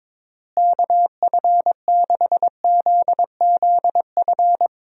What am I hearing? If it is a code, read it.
Morse code, heard as a keyed tone, about 22 wpm, 700 Hz. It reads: KF6ZZF
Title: USA Call Signs - 6 Characters - Repeated 3x - 22wpm